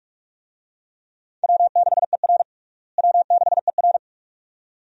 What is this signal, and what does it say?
Morse code, heard as a keyed tone, about 47 wpm, 700 Hz. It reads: W6ER W6ER
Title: USA Call Signs - 4 Characters - Repeated 2x - 45wpm